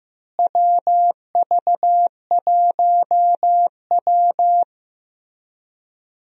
Morse code, heard as a keyed tone, about 15 wpm, 700 Hz. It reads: WV1W
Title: USA Call Signs - 4 Characters - Repeated 3x - 15wpm